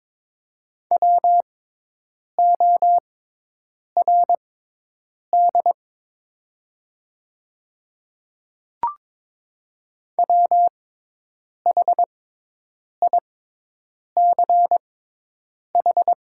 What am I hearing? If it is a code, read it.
Morse code, heard as a keyed tone, about 22 wpm, spaced as for 8 wpm, 700 Hz. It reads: WORD WHICH
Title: 6x Character Spacing - Top 100 Words - 22wpm